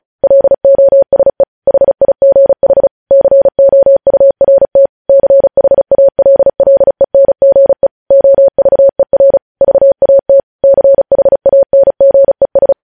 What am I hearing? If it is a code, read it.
Morse code, heard as a keyed tone, about 36 wpm, 550 Hz. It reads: LOSE HIGH COURT CHALLENGE OVER VAT CHANGES
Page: news headlines in morse code at 35 wpm